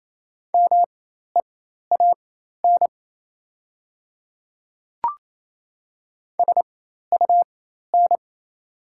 Morse code, heard as a keyed tone, about 28 wpm, spaced as for 13 wpm, 700 Hz. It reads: MEAN SUN
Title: 4x Character Spacing - Top 300 Words - 28wpm